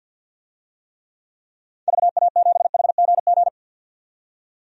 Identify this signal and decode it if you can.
Morse code, heard as a keyed tone, about 52 wpm, 700 Hz. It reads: VA7HGG
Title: DXCC-100 Call Signs - 6 Characters - 50wpm